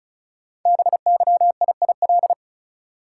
Morse code, heard as a keyed tone, about 36 wpm, 700 Hz. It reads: BYIIL
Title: Code Groups - 5 Letters - 35wpm